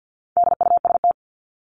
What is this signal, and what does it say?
Morse code, heard as a keyed tone, about 50 wpm, 700 Hz. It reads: DUST
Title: Top 1751-2000 Words 50wpm